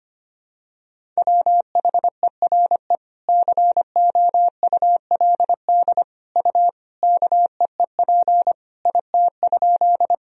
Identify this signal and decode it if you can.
Morse code, heard as a keyed tone, about 25 wpm, 700 Hz. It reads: WHERE COULD U KEEP IT?
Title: Sentences from Top 300 Words and Abbreviations 25wpm